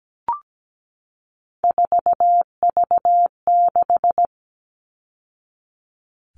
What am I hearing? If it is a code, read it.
Morse code, heard as a keyed tone, about 17 wpm, 700 Hz. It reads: 4V6
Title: All Cut Numbers - 3-Digit 17wpm